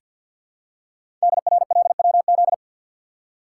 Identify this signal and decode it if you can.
Morse code, heard as a keyed tone, about 48 wpm, 700 Hz. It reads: DRLWZ
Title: Code Groups - 5 Letters - 50wpm